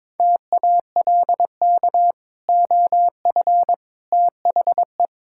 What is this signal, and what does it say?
Morse code, heard as a keyed tone, about 22 wpm, 700 Hz. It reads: TALK OF THE
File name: practice / 22wpm / Build A Phrase - 5-Word Phrases - 22wpm